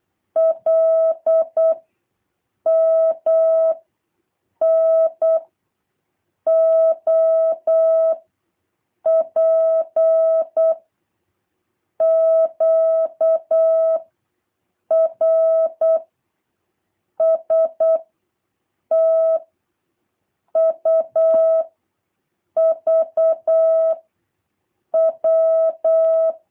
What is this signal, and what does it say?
Morse code, heard as a keyed tone, about 8 wpm, 650 Hz. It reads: L M N O P Q R S T U V W